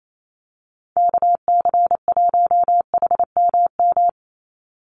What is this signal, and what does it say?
Morse code, heard as a keyed tone, about 28 wpm, 700 Hz. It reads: KC1HMM
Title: USA Call Signs 28wpm